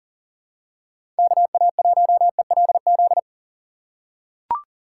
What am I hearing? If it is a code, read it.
Morse code, heard as a keyed tone, about 40 wpm, 700 Hz. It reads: KA1ELZ
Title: USA Call Signs - 6 Characters - 3x Character Spacing - 40wpm